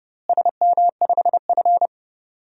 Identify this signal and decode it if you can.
Morse code, heard as a keyed tone, about 32 wpm, 700 Hz. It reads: SM5F
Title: DXCC-100 Call Signs - 4 Characters - Repeated 3x - 30wpm